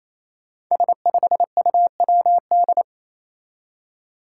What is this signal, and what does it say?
Morse code, heard as a keyed tone, about 27 wpm, 700 Hz. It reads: S5UWD
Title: Five Challenging Letter-Number Combinations 28wpm